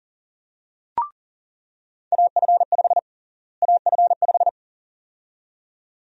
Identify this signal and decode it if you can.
Morse code, heard as a keyed tone, about 40 wpm, 700 Hz. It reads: AF5 AF5
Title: USA Call Sign Prefixes - Repeated 2x - 40wpm